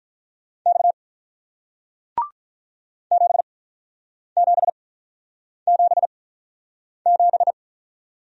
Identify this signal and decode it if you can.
Morse code, heard as a keyed tone, about 48 wpm, spaced as for 9 wpm, 700 Hz. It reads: X 7777